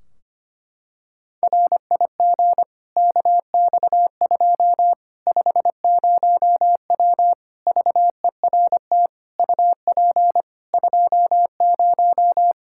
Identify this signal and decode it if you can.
Morse code, heard as a keyed tone, about 25 wpm, 700 Hz. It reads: RIG KX2 50W VERT UP 20
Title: Practice QSO - Exchange 2 - Info - 25wpm